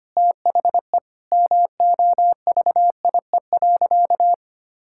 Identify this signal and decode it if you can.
Morse code, heard as a keyed tone, about 25 wpm, 700 Hz. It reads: THE MOVIE.
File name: practice / 25wpm / Sentences with 3-Word Phrases - 25wpm